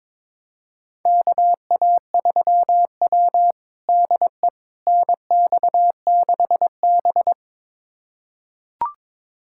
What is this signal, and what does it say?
Morse code, heard as a keyed tone, about 22 wpm, 700 Hz. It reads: KA3W DE NX6B
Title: USA Callsign DE Callsign - 4 Characters - 22wpm